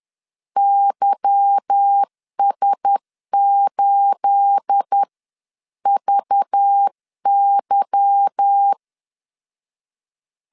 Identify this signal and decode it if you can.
Morse code, heard as a keyed tone, about 10 wpm, spaced as for 9 wpm, 800 Hz. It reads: YS8VY
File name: LEZIONI / 9 28 10wpm 2842